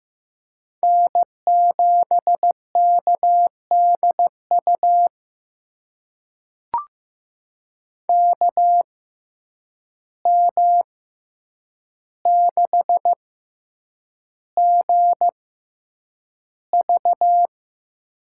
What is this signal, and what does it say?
Morse code, heard as a keyed tone, about 15 wpm, spaced as for 8 wpm, 700 Hz. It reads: N7KDU KM6GV
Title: USA Call Signs - 5 Characters - 6x Character Spacing - 15wpm